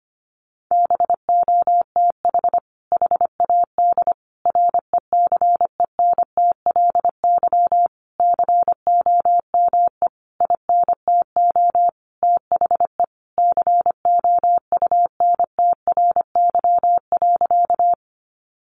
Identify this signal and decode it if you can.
Morse code, heard as a keyed tone, about 25 wpm, 700 Hz. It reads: BOTH HAD RECENTLY COME INTO THE COUNTRY.
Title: Sentences from Top 1200 Words 25wpm